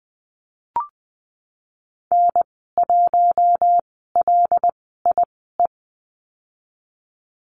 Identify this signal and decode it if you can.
Morse code, heard as a keyed tone, about 20 wpm, spaced as for 15 wpm, 700 Hz. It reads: N1LIE
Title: 076 Lesson 13g - Letter L - Call Signs 20wpm